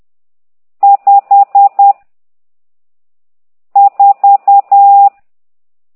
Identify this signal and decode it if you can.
Morse code, heard as a keyed tone, about 10 wpm, 800 Hz. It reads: 5 4
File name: ct2idw / exerc12